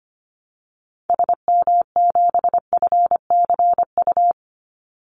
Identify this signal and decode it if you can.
Morse code, heard as a keyed tone, about 25 wpm, 700 Hz. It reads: SM7FCU
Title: Call Signs Repeated 2x - 25wpm